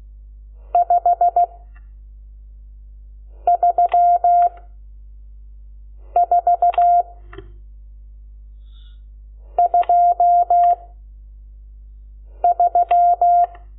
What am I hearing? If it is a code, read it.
Morse code, heard as a keyed tone, about 16 wpm, 700 Hz. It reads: 5 3 4 2 3
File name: Lecciones / CLASE Nº12